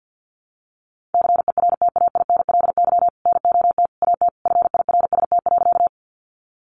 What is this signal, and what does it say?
Morse code, heard as a keyed tone, about 50 wpm, 700 Hz. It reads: CERTAINLY NOT AT FIRST.
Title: Sentences from Top 900 Words 50wpm